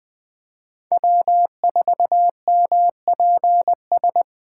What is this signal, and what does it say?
Morse code, heard as a keyed tone, about 20 wpm, 700 Hz. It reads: W4MPS
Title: 2023 SOTA Chaser Call Signs - 20wpm